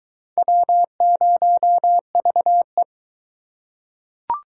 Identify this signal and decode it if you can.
Morse code, heard as a keyed tone, about 22 wpm, 700 Hz. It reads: W0VE